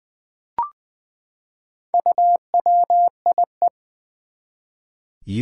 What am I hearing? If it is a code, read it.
Morse code, heard as a keyed tone, about 21 wpm, 700 Hz. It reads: UWIE
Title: Four Challenging Letter-Number Combinations 20wpm